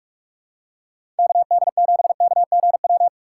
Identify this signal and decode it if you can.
Morse code, heard as a keyed tone, about 46 wpm, 700 Hz. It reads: KD7KGW
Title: USA Call Signs - 6 Characters - 3x Character Spacing - 45wpm